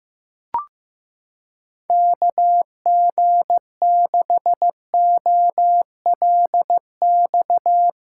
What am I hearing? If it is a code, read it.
Morse code, heard as a keyed tone, about 16 wpm, 700 Hz. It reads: KG6OLX
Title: USA Callsign DE Callsign - 6 Characters - 15wpm